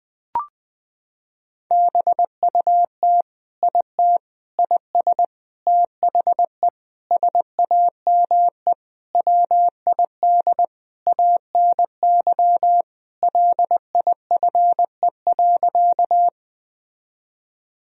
Morse code, heard as a keyed tone, about 20 wpm, 700 Hz. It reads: BUT IT IS THE SAME WID ANY LIFE.
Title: Sentences from Top 300 Words and Abbreviations 20wpm